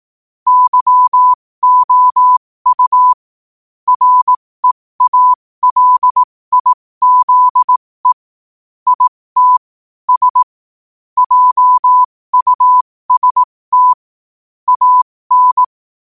Morse code, heard as a keyed tone, about 18 wpm, spaced as for 15 wpm, 1000 Hz. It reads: YOU REALIZE IT S JUST AN
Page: qotd-at-15-wpm